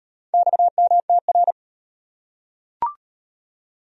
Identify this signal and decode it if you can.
Morse code, heard as a keyed tone, about 38 wpm, 700 Hz. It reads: XMTR